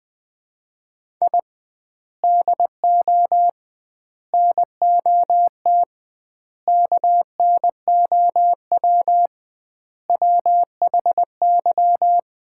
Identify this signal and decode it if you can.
Morse code, heard as a keyed tone, about 20 wpm, 700 Hz. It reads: I DO NOT KNOW WHY
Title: 2x Word-Spacing - Sets of 5 Words - 20wpm